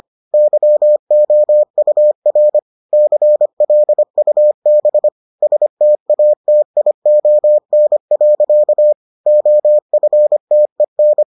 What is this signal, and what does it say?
Morse code, heard as a keyed tone, about 25 wpm, 600 Hz. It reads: YOUR CLUB STATION. OFTEN